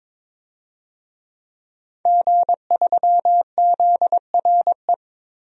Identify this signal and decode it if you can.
Morse code, heard as a keyed tone, about 22 wpm, 700 Hz. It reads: G3ZRE
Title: DXCC-100 Call Signs - 5 Characters - 22wpm